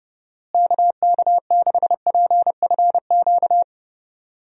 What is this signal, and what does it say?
Morse code, heard as a keyed tone, about 30 wpm, 700 Hz. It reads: KK6PFQ
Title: USA Call Signs - 6 Characters - Repeated 3x - 30wpm